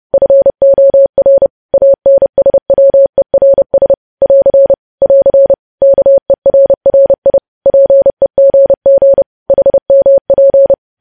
Morse code, heard as a keyed tone, about 30 wpm, 550 Hz. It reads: FOR ANSWERS <AR> <AR> KERRI PEGG HMP
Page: news headlines in morse code at 30 wpm